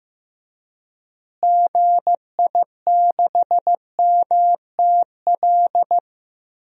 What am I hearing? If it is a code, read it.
Morse code, heard as a keyed tone, about 15 wpm, 700 Hz. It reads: GI6MTL
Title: DXCC-100 Call Signs - 6 Characters - Repeated 3x - 15wpm